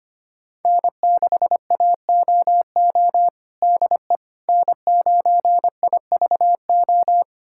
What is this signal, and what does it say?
Morse code, heard as a keyed tone, about 25 wpm, 700 Hz. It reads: N6AOO DE N9IVO
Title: USA Callsign DE Callsign - 5 Characters - 25wpm